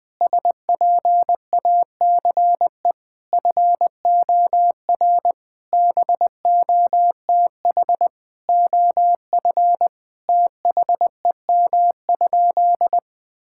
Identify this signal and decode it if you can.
Morse code, heard as a keyed tone, about 21 wpm, 700 Hz. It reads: SPACE FOR BOTH OF THEM?
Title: Sentences from Top 1000 Words - Review Part 5 - 20wpm